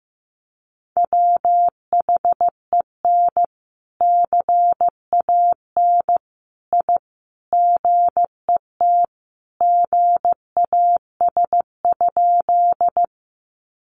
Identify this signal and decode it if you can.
Morse code, heard as a keyed tone, about 15 wpm, 700 Hz. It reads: WHEN CAN I GET GAS?
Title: Sentences from Top 1100 Words 15wpm